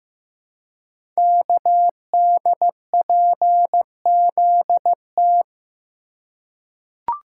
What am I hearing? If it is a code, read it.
Morse code, heard as a keyed tone, about 15 wpm, 700 Hz. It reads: KDPZT
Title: Code Groups - 5 Letters - 15wpm